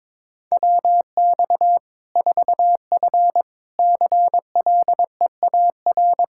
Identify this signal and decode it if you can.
Morse code, heard as a keyed tone, about 21 wpm, 700 Hz. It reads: WX 4F CLEAR
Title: Practice QSO - Exchange 2 - Info - 22wpm